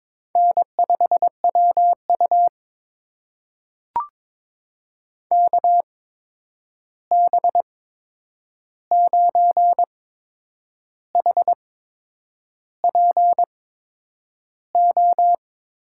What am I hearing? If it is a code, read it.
Morse code, heard as a keyed tone, about 23 wpm, spaced as for 8 wpm, 700 Hz. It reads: N5WU KB9HPO
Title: USA Call Signs - 8x Character Spacing - 22wpm